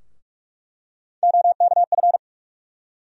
Morse code, heard as a keyed tone, about 46 wpm, 700 Hz. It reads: OKF
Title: USA Call Sign Postfixes - 3 Letters 45wpm